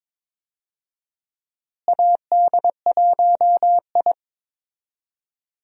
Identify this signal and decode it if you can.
Morse code, heard as a keyed tone, about 22 wpm, 700 Hz. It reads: AD1I
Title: USA Call Signs - 4 Characters - Repeated 2x - 22wpm